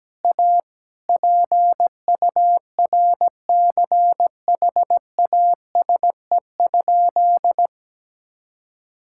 Morse code, heard as a keyed tone, about 17 wpm, 700 Hz. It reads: A PURCHASE?
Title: Sentences from Top 1000 Words - Review Part 6 - 17wpm